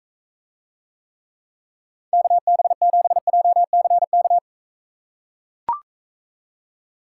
Morse code, heard as a keyed tone, about 45 wpm, 700 Hz. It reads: KB7JCK